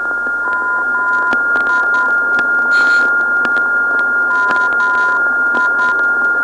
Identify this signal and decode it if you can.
Morse code, heard as a keyed tone, about 10 wpm, 1010 Hz, with noise. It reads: MI MI
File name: ndb zvok / MI-355khz